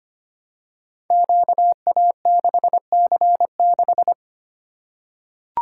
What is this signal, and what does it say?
Morse code, heard as a keyed tone, about 25 wpm, 700 Hz. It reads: QA6C6
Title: Code Groups - 5 Letter:Numbers - 25wpm